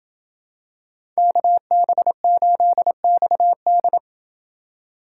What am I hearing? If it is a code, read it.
Morse code, heard as a keyed tone, about 27 wpm, 700 Hz. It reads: KB8XD